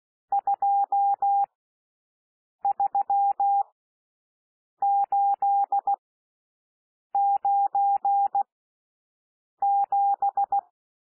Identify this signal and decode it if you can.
Morse code, heard as a keyed tone, about 16 wpm, spaced as for 6 wpm, 800 Hz. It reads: 23897
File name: SOUND NUMBERS 16 wpm